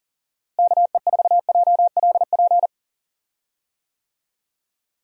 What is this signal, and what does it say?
Morse code, heard as a keyed tone, about 40 wpm, 700 Hz. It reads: KE4JLP
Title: USA Call Signs - 6 Characters - Repeated 3x - 40wpm